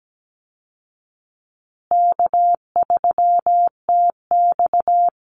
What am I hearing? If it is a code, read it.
Morse code, heard as a keyed tone, about 17 wpm, 700 Hz. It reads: K3TX
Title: Easy Call Signs Repeated 3x - 17wpm